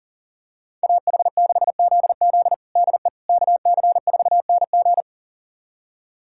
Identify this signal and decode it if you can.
Morse code, heard as a keyed tone, about 40 wpm, 700 Hz. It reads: AH6ZZ DE KC4NG
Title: USA Callsign DE Callsign - 5 Characters - 40wpm